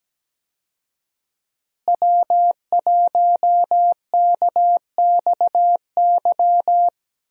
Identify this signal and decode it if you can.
Morse code, heard as a keyed tone, about 17 wpm, 700 Hz. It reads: W1KXY
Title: USA Call Signs - 5 Characters - Repeated 3x - 17wpm